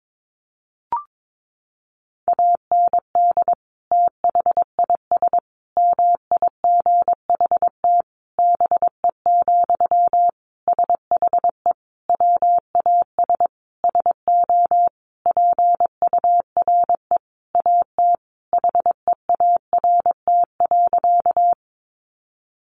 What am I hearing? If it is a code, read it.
Morse code, heard as a keyed tone, about 22 wpm, 700 Hz. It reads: AND THIS MIGHT BE, SHE WAS SO PURE AT HEART.
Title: Sentences from Top 1500 Words 22wpm